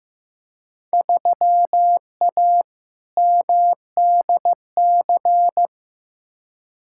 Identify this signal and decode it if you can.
Morse code, heard as a keyed tone, about 15 wpm, 700 Hz. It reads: 3A MDC
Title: Field Day Exchanges - 15wpm